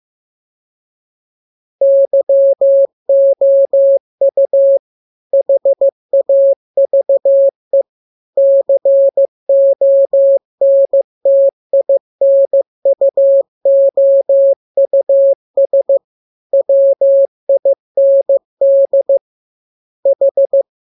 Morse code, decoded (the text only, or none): YOU HAVE CONTINUOUS WIND H